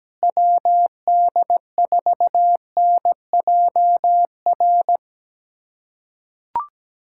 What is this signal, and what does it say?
Morse code, heard as a keyed tone, about 17 wpm, 700 Hz. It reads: WD4NJR
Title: USA Callsign DE Callsign - 6 Characters - 17wpm